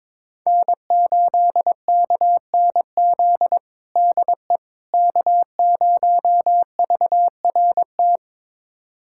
Morse code, heard as a keyed tone, about 21 wpm, 700 Hz. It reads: N8KNZ DE K0VRT
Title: USA Callsign DE Callsign - 5 Characters - 22wpm